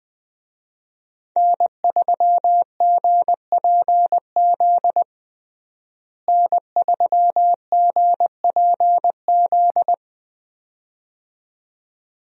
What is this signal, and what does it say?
Morse code, heard as a keyed tone, about 21 wpm, 700 Hz. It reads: N3GPZ N3GPZ
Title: USA Call Signs - 5 Characters - Repeated 2x - 20wpm